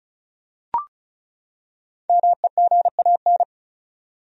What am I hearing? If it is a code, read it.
Morse code, heard as a keyed tone, about 34 wpm, 700 Hz. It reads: MEGAN
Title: USA Common Names - Max 5 Letters - 35wpm